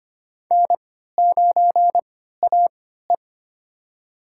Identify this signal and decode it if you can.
Morse code, heard as a keyed tone, about 25 wpm, spaced as for 14 wpm, 700 Hz. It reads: N9AE
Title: USA Call Signs - 4 Characters - 3x Character Spacing - 25wpm